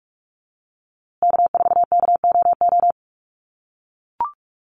Morse code, heard as a keyed tone, about 44 wpm, 700 Hz. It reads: K4KOO
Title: USA Call Signs Repeated 2x 45wpm